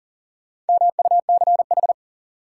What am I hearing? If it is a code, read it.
Morse code, heard as a keyed tone, about 40 wpm, 700 Hz. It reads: MUCH
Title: Sentences from Top 200 Words - New Word Drill - 40wpm